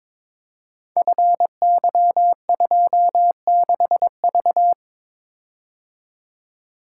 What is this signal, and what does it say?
Morse code, heard as a keyed tone, about 22 wpm, 700 Hz. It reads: FY26V
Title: Code Groups - 5 Letter:Numbers - 22wpm